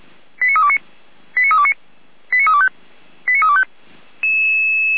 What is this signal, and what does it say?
Five-tone selective call (ZVEI), heard as quick runs of five tones, tones of about 70 ms. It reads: 89329 89329 89325 89325